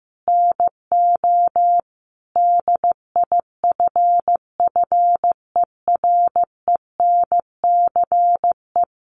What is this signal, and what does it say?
Morse code, heard as a keyed tone, about 15 wpm, 700 Hz. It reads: NO DIFFERENCE
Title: Sentences from Top 800 Words 15wpm